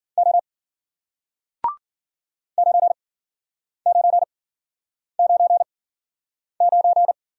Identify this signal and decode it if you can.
Morse code, heard as a keyed tone, about 50 wpm, spaced as for 9 wpm, 700 Hz. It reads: O 9999